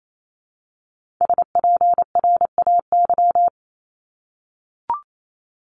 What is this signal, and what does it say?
Morse code, heard as a keyed tone, about 28 wpm, 700 Hz. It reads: SPRAY
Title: Multi-letter Phonemes Repeated 3x 28wpm